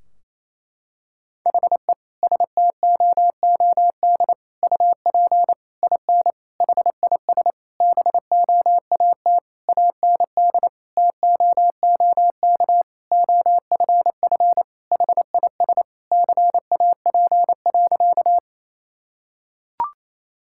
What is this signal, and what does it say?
Morse code, heard as a keyed tone, about 28 wpm, 700 Hz. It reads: HE STOOD UP IN HIS BOAT AND TOOK OFF HIS CAP.
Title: Sentences from Top 1600 Words 28wpm